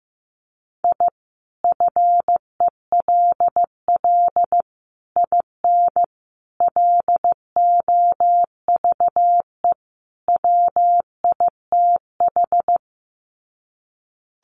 Morse code, heard as a keyed tone, about 15 wpm, 700 Hz. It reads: I FELL IN LOVE WITH
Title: Sets of 5 Words Encore 15wpm